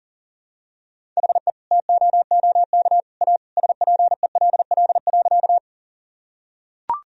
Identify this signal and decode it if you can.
Morse code, heard as a keyed tone, about 40 wpm, 700 Hz. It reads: HE TOOK A SPELL.
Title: Sentences from Top 1000 Words - Review Part 1 - 40wpm